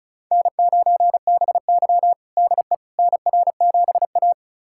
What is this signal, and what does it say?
Morse code, heard as a keyed tone, about 36 wpm, 690 Hz. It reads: N9BY DE NR7A
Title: USA Callsign DE Callsign - 4 Characters - 35wpm